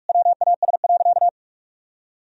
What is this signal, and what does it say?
Morse code, heard as a keyed tone, about 47 wpm, 700 Hz. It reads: WAS.
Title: Sentences from Top 1000 Words - Review Part 1 - 45wpm